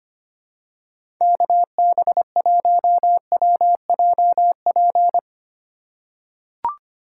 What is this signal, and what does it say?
Morse code, heard as a keyed tone, about 25 wpm, 700 Hz. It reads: KB1WJP
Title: USA Call Signs - 6 Characters - 4x Character Spacing - 25wpm